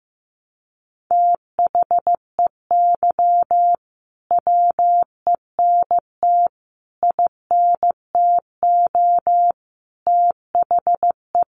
Morse code, heard as a keyed tone, about 15 wpm, 700 Hz. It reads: THEY WENT INTO THE
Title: Sentences from Top 900 Words 15wpm